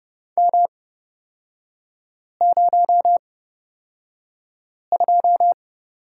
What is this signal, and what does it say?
Morse code, heard as a keyed tone, about 30 wpm, spaced as for 5 wpm, 700 Hz. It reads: M02
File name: Lesson 36e - BT - Three Characters 30wpm